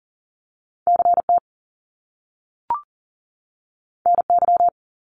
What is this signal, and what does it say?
Morse code, heard as a keyed tone, about 41 wpm, 700 Hz. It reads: CT NY
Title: Top Two Letter Combinations in Words 40wpm